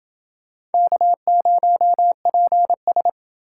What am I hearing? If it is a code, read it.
Morse code, heard as a keyed tone, about 27 wpm, 700 Hz. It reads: K0PS